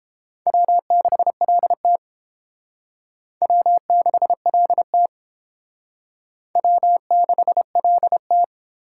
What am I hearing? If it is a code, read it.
Morse code, heard as a keyed tone, about 29 wpm, 700 Hz. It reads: W6LT W6LT W6LT